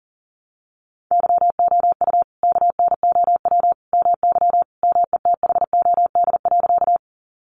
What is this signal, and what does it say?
Morse code, heard as a keyed tone, about 37 wpm, 700 Hz. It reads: YOU KNOW MY METHOD.
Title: Sentences from Top 600 Words 40wpm